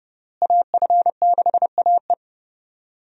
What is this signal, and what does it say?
Morse code, heard as a keyed tone, about 30 wpm, 700 Hz. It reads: AF6AE
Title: USA Call Signs - 5 Characters - Repeated 3x - 30wpm